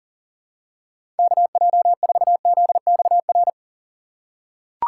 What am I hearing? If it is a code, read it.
Morse code, heard as a keyed tone, about 38 wpm, 700 Hz. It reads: KJ4ZXR
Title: USA Call Signs - 6 Characters - 2x Character Spacing - 40wpm